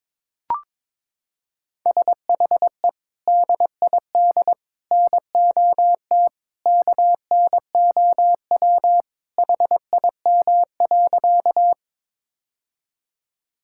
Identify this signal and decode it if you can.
Morse code, heard as a keyed tone, about 22 wpm, 700 Hz. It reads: SHE DID NOT KNOW HIM.
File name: practice / 22wpm / Sentences with 3-Word Phrases - 22wpm